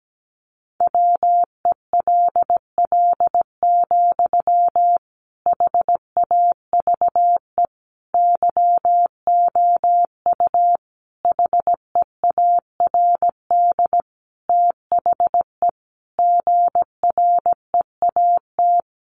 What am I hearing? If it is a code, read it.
Morse code, heard as a keyed tone, about 17 wpm, 700 Hz. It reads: WELL, HAVE YOU HEARD THE GREAT
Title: Sentences from Top 800 Words 17wpm